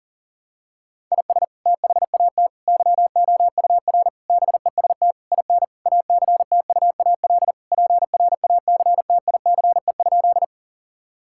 Morse code, heard as a keyed tone, about 39 wpm, 700 Hz. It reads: IS THAT YOUR BEST IN ACTUAL PRACTICE?
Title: Sentences from Top 1000 Words - Review Part 5 - 40wpm